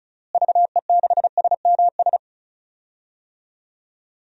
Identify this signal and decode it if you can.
Morse code, heard as a keyed tone, about 36 wpm, 700 Hz. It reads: VE6SMS
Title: DXCC-100 Call Signs - 6 Characters - Repeated 2x - 35wpm